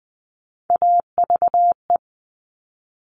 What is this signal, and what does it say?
Morse code, heard as a keyed tone, about 20 wpm, 700 Hz. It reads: AVE
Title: Top Three Letter Combinations in Words 20wpm